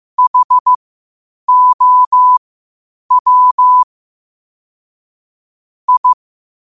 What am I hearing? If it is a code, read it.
Morse code, heard as a keyed tone, about 15 wpm, spaced as for 8 wpm, 1000 Hz. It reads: HOW I
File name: qotdfiles / 8wpm